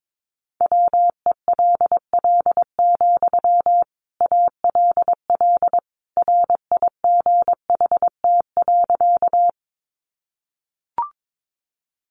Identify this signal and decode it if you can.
Morse code, heard as a keyed tone, about 22 wpm, 700 Hz. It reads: WELL, ALL RIGHT.
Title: Sentences from Top 200 Words 22wpm